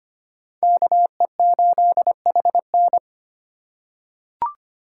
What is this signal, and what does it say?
Morse code, heard as a keyed tone, about 25 wpm, 700 Hz. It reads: KE8HN
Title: USA Call Signs - 5 Characters - 3x Character Spacing - 25wpm